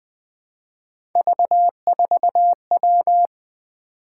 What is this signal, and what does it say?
Morse code, heard as a keyed tone, about 20 wpm, 700 Hz. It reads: V4W
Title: Three Letter-Number Combinations - 20wpm